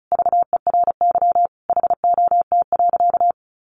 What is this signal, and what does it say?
Morse code, heard as a keyed tone, about 34 wpm, 700 Hz. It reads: VERY HOT.
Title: Sentences from Top 1500 Words 35wpm